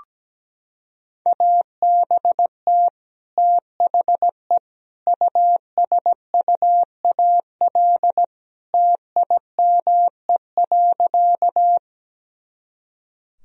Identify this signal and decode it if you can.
Morse code, heard as a keyed tone, about 17 wpm, 700 Hz. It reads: ABT THE USUAL TIME.
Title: Sentences from Top 400 Words and Abbreviations 17wpm